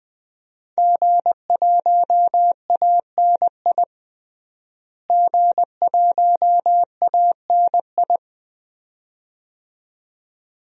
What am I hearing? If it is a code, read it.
Morse code, heard as a keyed tone, about 20 wpm, 700 Hz. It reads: G1ANI G1ANI
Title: DXCC-100 Call Signs - 5 Characters - Repeated 2x - 20wpm